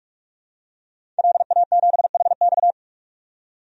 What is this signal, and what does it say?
Morse code, heard as a keyed tone, about 44 wpm, 690 Hz. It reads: LA7HX
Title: DXCC-100 Call Signs - 5 Characters - 45wpm